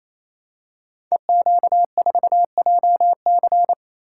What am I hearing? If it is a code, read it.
Morse code, heard as a keyed tone, about 27 wpm, 700 Hz. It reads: EQ4JC
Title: Code Groups - 5 Letter:Numbers - 28wpm